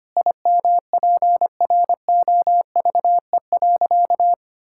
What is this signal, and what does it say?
Morse code, heard as a keyed tone, about 25 wpm, 700 Hz. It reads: IMPROVE.
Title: Sentences from Top 1000 Words - Review Part 5 - 25wpm